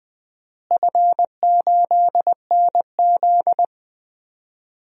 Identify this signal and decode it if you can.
Morse code, heard as a keyed tone, about 21 wpm, 700 Hz. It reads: F8NZ
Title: DXCC-100 Call Signs - 4 Characters - Repeated 2x - 20wpm